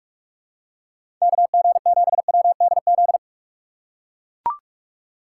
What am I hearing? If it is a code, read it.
Morse code, heard as a keyed tone, about 42 wpm, 700 Hz. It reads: KG7WDZ
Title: USA Call Signs - 6 Characters - 4x Character Spacing - 45wpm